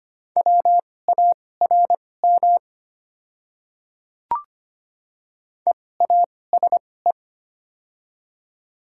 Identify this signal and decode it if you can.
Morse code, heard as a keyed tone, about 25 wpm, spaced as for 18 wpm, 700 Hz. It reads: WARM EASE
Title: 2x Character Spacing - Top 500 Words - 25wpm